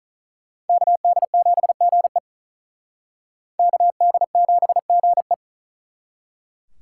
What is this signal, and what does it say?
Morse code, heard as a keyed tone, about 45 wpm, 700 Hz. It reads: KD7GE KD7GE